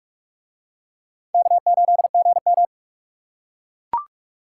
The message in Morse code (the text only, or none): K8GM